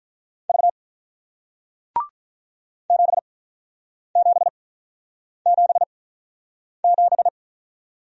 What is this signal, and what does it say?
Morse code, heard as a keyed tone, about 48 wpm, spaced as for 9 wpm, 700 Hz. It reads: V 7777